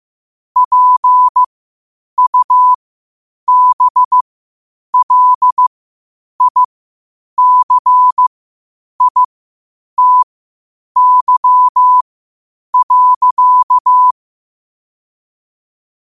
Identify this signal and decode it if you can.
Morse code, heard as a keyed tone, about 15 wpm, spaced as for 8 wpm, 1000 Hz. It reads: PUBLICITY.
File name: qotdfiles / 8wpm